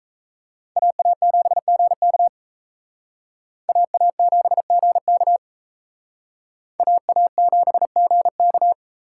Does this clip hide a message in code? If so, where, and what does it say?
Yes: Morse code from 0.8 to 8.7 s, heard as a keyed tone, about 35 wpm, 700 Hz. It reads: AA7GK AA7GK AA7GK